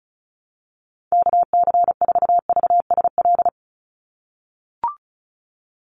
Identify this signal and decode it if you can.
Morse code, heard as a keyed tone, about 36 wpm, 700 Hz. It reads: KC4VSL
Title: USA Call Signs 35wpm